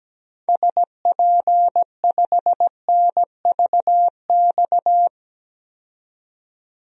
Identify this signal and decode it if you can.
Morse code, heard as a keyed tone, about 17 wpm, 700 Hz. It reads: SP5NVX
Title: DXCC-100 Call Signs - 6 Characters - 17wpm